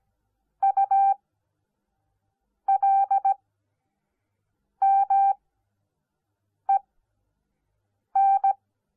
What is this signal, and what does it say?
Morse code, heard as a keyed tone, about 17 wpm, spaced as for 5 wpm, 780 Hz. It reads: ULMEN